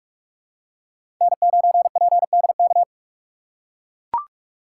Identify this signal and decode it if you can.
Morse code, heard as a keyed tone, about 44 wpm, 700 Hz. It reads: N9PDK
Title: USA Call Signs - 5 Characters - 6x Character Spacing - 45wpm